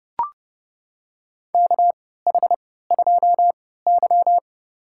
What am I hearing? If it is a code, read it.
Morse code, heard as a keyed tone, about 30 wpm, spaced as for 17 wpm, 700 Hz. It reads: KH2Y
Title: USA Call Signs - 3x Character Spacing - 30wpm